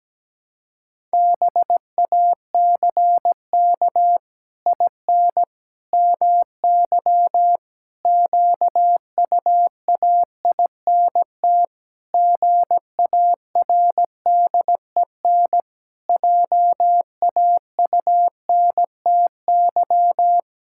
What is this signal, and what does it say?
Morse code, heard as a keyed tone, about 17 wpm, 700 Hz. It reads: BACK IN MY QUAINT GARDEN JAUNTY
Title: Pangram Challenge v2 - 17wpm